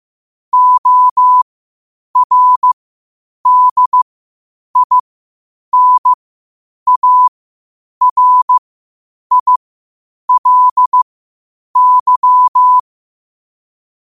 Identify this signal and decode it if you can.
Morse code, heard as a keyed tone, about 15 wpm, spaced as for 8 wpm, 1000 Hz. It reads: ORDINARILY
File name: qotdfiles / 8wpm